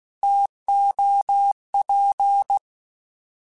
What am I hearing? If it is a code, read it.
Morse code, heard as a keyed tone, about 16 wpm, 800 Hz. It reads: TOP